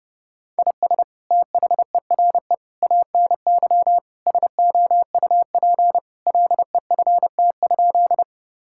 Morse code, heard as a keyed tone, about 31 wpm, 700 Hz. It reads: IS THERE ANY SOUP LEFT?
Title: Sentences from Top 1000 Words - Review Part 6 - 30wpm